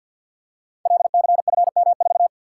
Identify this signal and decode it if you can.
Morse code, heard as a keyed tone, about 52 wpm, 700 Hz. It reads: LCFM4
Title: Code Groups - 5 Letter:Numbers - 50wpm